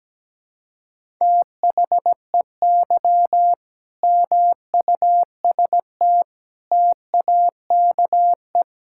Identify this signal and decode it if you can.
Morse code, heard as a keyed tone, about 17 wpm, 700 Hz. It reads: THEY MUST TAKE
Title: Sentences from Top 1000 Words - Review Part 6 - 17wpm